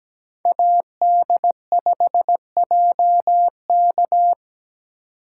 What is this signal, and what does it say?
Morse code, heard as a keyed tone, about 20 wpm, 700 Hz. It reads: AD5JK
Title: Speed Racing - USA 2x2 Call Signs 17wpm